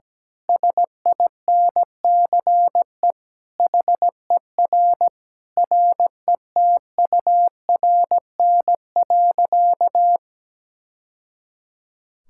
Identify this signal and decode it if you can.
Morse code, heard as a keyed tone, about 17 wpm, 700 Hz. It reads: SINCE HER RETURN.
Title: Sentences from Top 1000 Words - Review Part 6 - 17wpm